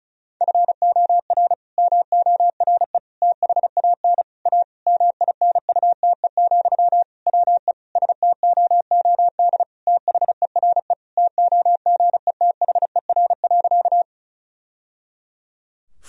Morse code, heard as a keyed tone, about 34 wpm, 700 Hz. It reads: FOR MORE THAN A MINUTE, WE STOOD THERE TOGETHER.
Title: Sentences from Top 1000 Words - Review Part 2 - 35wpm